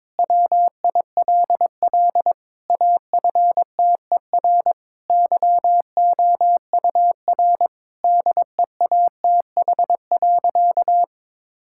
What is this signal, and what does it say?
Morse code, heard as a keyed tone, about 22 wpm, 700 Hz. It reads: WILL AFTER YOUR DEATH.
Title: Sentences from Top 1600 Words 22wpm